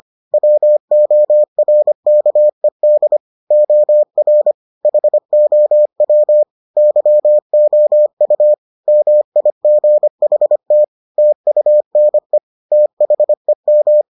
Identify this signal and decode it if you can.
Morse code, heard as a keyed tone, about 25 wpm, 600 Hz. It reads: WORKED OR HOW YOU MIGHT TUNE THEM